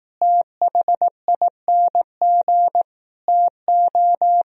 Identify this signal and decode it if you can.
Morse code, heard as a keyed tone, about 18 wpm, 700 Hz. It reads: THING TO
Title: Speed-Racing - Sets of 5 Words 15wpm